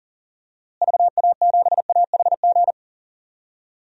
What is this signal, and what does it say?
Morse code, heard as a keyed tone, about 40 wpm, 700 Hz. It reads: VA7AHG
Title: DXCC-100 Call Signs - 6 Characters - Repeated 2x - 40wpm